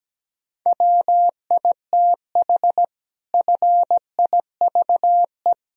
Morse code, heard as a keyed tone, about 17 wpm, 700 Hz. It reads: WITH FIVE
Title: Sentences from Top 1000 Words - Review Part 3 - 17wpm